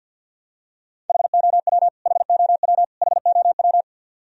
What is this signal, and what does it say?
Morse code, heard as a keyed tone, about 52 wpm, 700 Hz. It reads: HOW HOW HOW
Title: Top 100 Words Repeated 50wpm